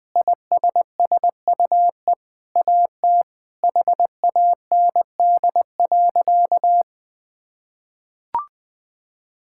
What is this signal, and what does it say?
Morse code, heard as a keyed tone, about 21 wpm, 700 Hz. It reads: ISSUE AT HAND.
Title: Sentences from Top 1000 Words - Review Part 6 - 20wpm